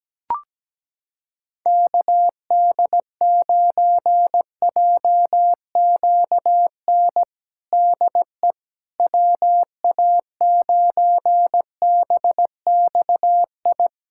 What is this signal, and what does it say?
Morse code, heard as a keyed tone, about 17 wpm, 700 Hz. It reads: KD9JQN DE WA9BXI
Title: Practice QSO - Exchange 4 - Ending - 17wpm